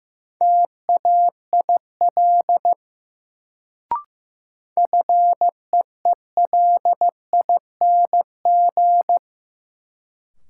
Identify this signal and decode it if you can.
Morse code, heard as a keyed tone, about 15 wpm, 700 Hz. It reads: TAIL FEELING
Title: Sentences from Top 1000 Words - Review Part 3 - New Word Drill - 15wpm